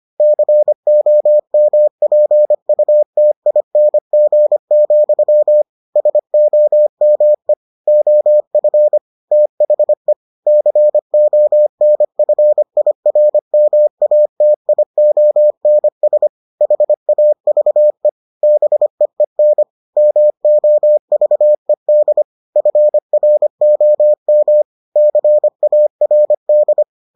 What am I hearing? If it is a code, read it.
Morse code, heard as a keyed tone, about 25 wpm, 600 Hz. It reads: COMPUTING, SOME OF THE CONFIRMATIONS HAVE BEEN MOVED FROM CARD